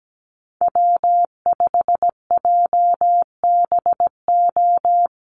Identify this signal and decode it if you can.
Morse code, heard as a keyed tone, about 17 wpm, 700 Hz. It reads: W5JBO
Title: Call Signs Repeated 2x - 17wpm